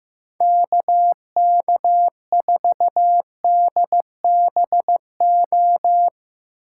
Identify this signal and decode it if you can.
Morse code, heard as a keyed tone, about 16 wpm, 700 Hz. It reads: KK4DBO
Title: Speed Racing - USA 2x3 Call Signs 15wpm